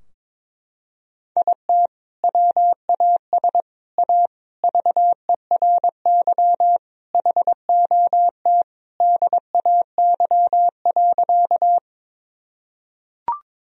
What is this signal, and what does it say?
Morse code, heard as a keyed tone, about 22 wpm, 700 Hz. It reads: IT WAS A VERY HOT DAY.
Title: Sentences from Top 1000 Words - Review Part 3 - 22wpm